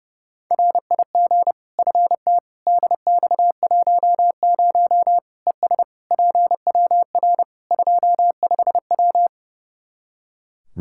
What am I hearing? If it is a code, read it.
Morse code, heard as a keyed tone, about 30 wpm, 700 Hz. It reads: RIG FT DX10 ES PWR 25W
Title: Practice QSO - Exchange 2 - Info - 30wpm